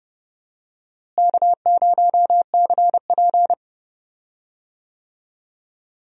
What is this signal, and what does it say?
Morse code, heard as a keyed tone, about 30 wpm, 700 Hz. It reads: K0CP
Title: USA Call Signs - 4 Characters - Repeated 2x - 30wpm